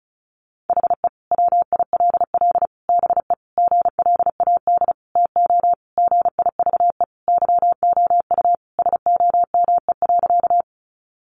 Morse code, heard as a keyed tone, about 34 wpm, 700 Hz. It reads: HE WILL BE GLAD TO GIVE YOU SOME.
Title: Sentences from Top 800 Words 35wpm